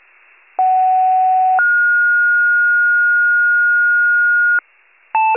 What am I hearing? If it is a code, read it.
Two-tone page, heard as a one-second tone, then a three-second tone, about 750 then 1430 Hz. It is heard from 0.6 to 4.6 s.